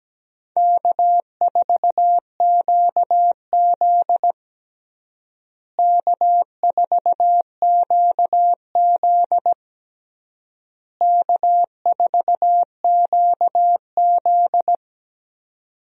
Morse code, heard as a keyed tone, about 17 wpm, 700 Hz. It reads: K4QZ K4QZ K4QZ
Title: USA Call Signs - 4 Characters - Repeated 3x - 17wpm